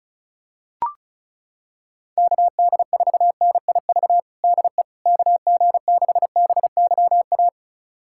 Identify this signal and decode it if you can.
Morse code, heard as a keyed tone, about 36 wpm, 700 Hz. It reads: KD4NIV DE KG6BYA
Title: USA Callsign DE Callsign 35wpm